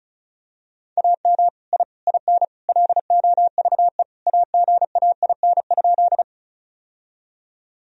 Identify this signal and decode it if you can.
Morse code, heard as a keyed tone, about 36 wpm, 700 Hz. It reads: AM I IN LOVE AGAIN?
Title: Sentences from Top 1000 Words - Review Part 2 - 35wpm